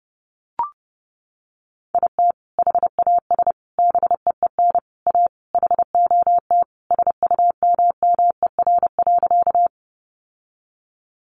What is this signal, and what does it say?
Morse code, heard as a keyed tone, about 28 wpm, 700 Hz. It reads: IT HAS BEEN A HOT SUMMER.
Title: Sentences from Top 1100 Words 30wpm